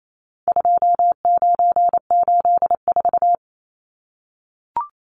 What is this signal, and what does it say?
Morse code, heard as a keyed tone, about 28 wpm, 700 Hz. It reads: 2984
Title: Numbers - 4-Digit 28wpm